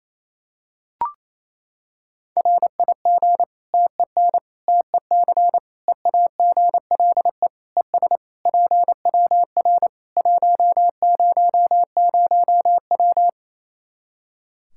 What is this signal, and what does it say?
Morse code, heard as a keyed tone, about 28 wpm, 700 Hz. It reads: RIG TEN TEC EAGLE ES PWR 100W
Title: Practice QSO - Exchange 2 - Info - 28wpm